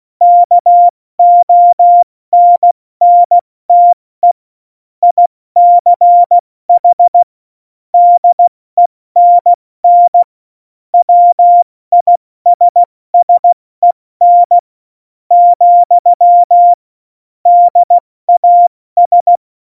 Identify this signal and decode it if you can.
Morse code, heard as a keyed tone, about 16 wpm, 700 Hz. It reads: KONNTE ICH DENN WISSEN , DAS
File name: Morsekurs / Witze / Witz 27 March 2018, 70 F80